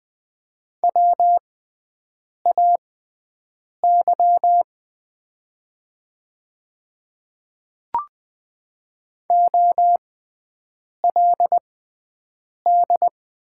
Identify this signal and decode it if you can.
Morse code, heard as a keyed tone, about 20 wpm, spaced as for 7 wpm, 700 Hz. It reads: WAY OLD
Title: 6x Character Spacing - Top 200 Words - 20wpm